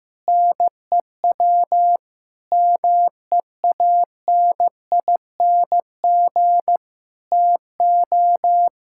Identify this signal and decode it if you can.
Morse code, heard as a keyed tone, about 15 wpm, 700 Hz. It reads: NEW MEANING TO
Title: Sentences from Top 1600 Words 15wpm